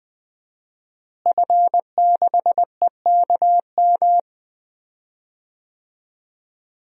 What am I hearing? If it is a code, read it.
Morse code, heard as a keyed tone, about 20 wpm, 700 Hz. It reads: F6EKM
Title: DXCC-100 Call Signs - 5 Characters - Repeated 2x - 20wpm